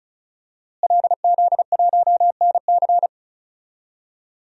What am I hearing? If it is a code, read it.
Morse code, heard as a keyed tone, about 34 wpm, 700 Hz. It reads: LZ1NC
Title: DXCC-100 Call Signs - 5 Characters - Repeated 2x - 35wpm